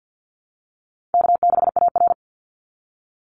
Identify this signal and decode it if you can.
Morse code, heard as a keyed tone, about 50 wpm, 700 Hz. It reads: K6AR
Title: Easy Call Signs - 50wpm